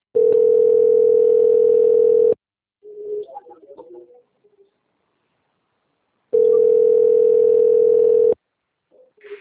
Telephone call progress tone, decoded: ringback tone

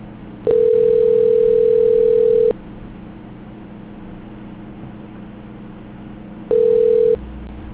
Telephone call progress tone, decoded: ringback tone

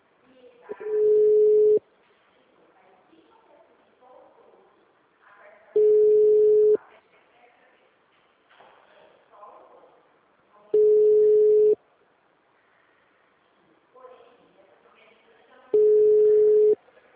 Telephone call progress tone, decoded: ringback tone

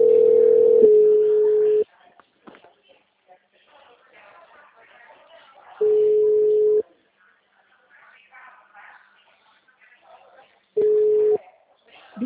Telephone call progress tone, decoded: ringback tone

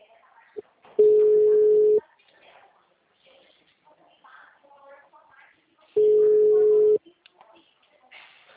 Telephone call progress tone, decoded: ringback tone

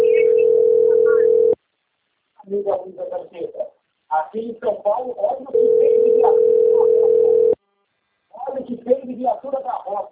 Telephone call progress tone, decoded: ringback tone

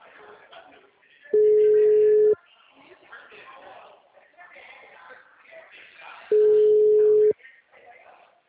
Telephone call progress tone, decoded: ringback tone

